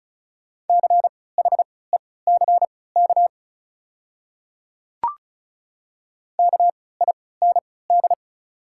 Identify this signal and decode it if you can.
Morse code, heard as a keyed tone, about 36 wpm, spaced as for 20 wpm, 690 Hz. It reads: CHECK KIND
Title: 3x Character Spacing - Top 500 Words - 35wpm